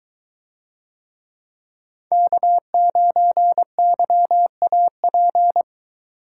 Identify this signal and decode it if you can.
Morse code, heard as a keyed tone, about 22 wpm, 700 Hz. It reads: K9YAP